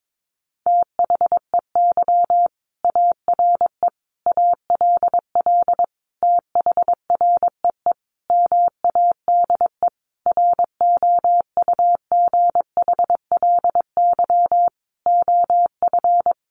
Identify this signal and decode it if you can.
Morse code, heard as a keyed tone, about 22 wpm, 700 Hz. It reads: THEY ARE ALL THREE MADE ROUGHLY OF
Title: Sentences from Top 1500 Words 22wpm